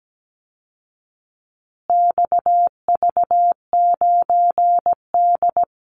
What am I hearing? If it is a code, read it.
Morse code, heard as a keyed tone, about 17 wpm, 700 Hz. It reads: XV9D
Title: Call Signs Repeated 3x - 17wpm